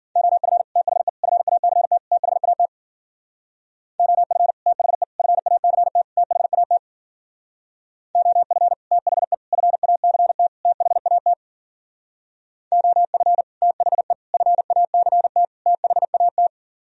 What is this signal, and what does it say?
Morse code, heard as a keyed tone, about 50 wpm, 700 Hz. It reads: OF THE FACT THAT OF THE FACT THAT OF THE FACT THAT OF THE FACT THAT
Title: Speed-Racing - Sets of 4 Words 40wpm